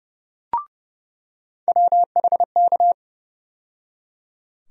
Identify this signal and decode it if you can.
Morse code, heard as a keyed tone, about 30 wpm, 700 Hz. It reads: WHK
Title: USA Call Sign Postfixes - 3 Letters 30wpm